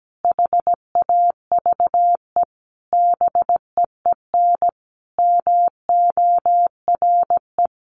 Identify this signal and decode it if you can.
Morse code, heard as a keyed tone, about 17 wpm, 700 Hz. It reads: HAVE BEEN MORE
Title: Sentences from Top 1300 Words 17wpm